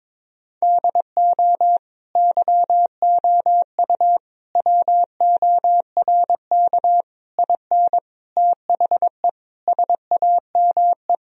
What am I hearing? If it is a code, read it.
Morse code, heard as a keyed tone, about 22 wpm, 700 Hz. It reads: DO YOU WORK IN THE SAME
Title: Sentences from Top 1000 Words - Review Part 4 - 22wpm